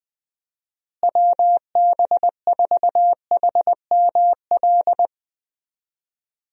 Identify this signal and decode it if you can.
Morse code, heard as a keyed tone, about 21 wpm, 700 Hz. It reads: WB4HML
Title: USA Call Signs - 6 Characters - Repeated 2x - 20wpm